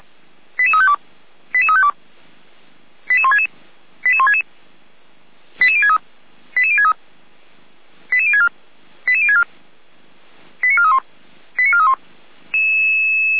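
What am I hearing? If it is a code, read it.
Five-tone selective call (ZVEI), heard as quick runs of five tones, tones of about 70 ms. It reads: 88352 88352 88166 88166 88063 88063 88064 88064 89431 89431
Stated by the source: Five-tone ZVEI selective call received by SDR.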